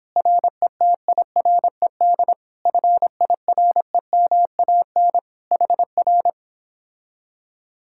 Morse code, heard as a keyed tone, about 29 wpm, 700 Hz. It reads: RETIRED FIREMAN HR